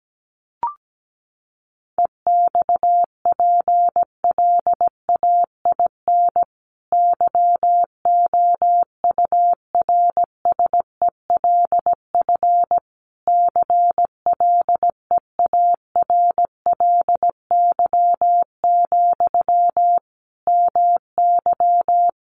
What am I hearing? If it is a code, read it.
Morse code, heard as a keyed tone, about 17 wpm, 700 Hz. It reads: EXPLAIN YOURSELF CLEARLY, MY
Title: Sentences from Top 1300 Words 17wpm